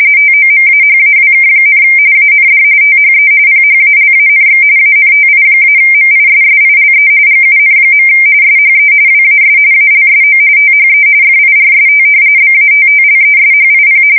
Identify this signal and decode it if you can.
RTTY, about 45.45 baud. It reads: TOMAHAWK, AND THE PHALANX ANTI-MISSILE GUN. SHE WAS ALSO THE FIRST BATTLESHIP
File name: fsk170 / battleshipnj